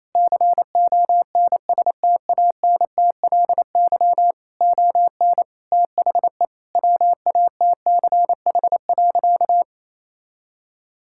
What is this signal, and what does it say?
Morse code, heard as a keyed tone, about 28 wpm, 700 Hz. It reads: CONSTANTLY ON THE WATCH.
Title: Sentences from Top 1600 Words 28wpm